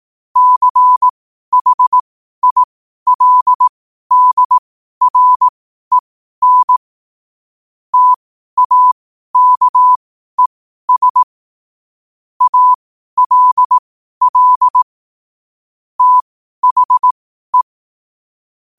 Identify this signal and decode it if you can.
Morse code, heard as a keyed tone, about 18 wpm, spaced as for 12 wpm, 1000 Hz. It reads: CHILDREN TAKES ALL THE
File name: qotdfiles / 12wpm